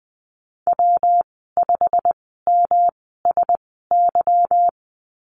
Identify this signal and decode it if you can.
Morse code, heard as a keyed tone, about 20 wpm, spaced as for 15 wpm, 700 Hz. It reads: W5MSY
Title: 173 Lesson 28g - Number 9 - Call Signs 20wpm